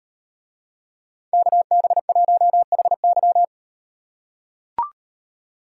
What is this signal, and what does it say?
Morse code, heard as a keyed tone, about 37 wpm, 700 Hz. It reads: KB1HY